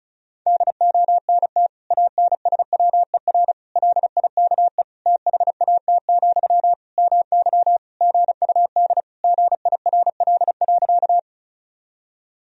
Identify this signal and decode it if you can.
Morse code, heard as a keyed tone, about 36 wpm, 700 Hz. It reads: DONT ANSWER LIKE THAT, MY GUD GIRL.
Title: Sentences from Top 400 Words and Abbreviations 35wpm